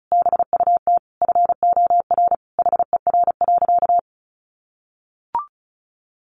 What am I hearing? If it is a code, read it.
Morse code, heard as a keyed tone, about 34 wpm, 700 Hz. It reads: BUT FOR HER.
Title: Sentences from Top 1400 Words 35wpm